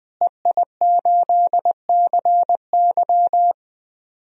Morse code, heard as a keyed tone, about 20 wpm, 700 Hz. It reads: EI8CY
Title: DXCC-100 Call Signs - 5 Characters - Repeated 3x - 20wpm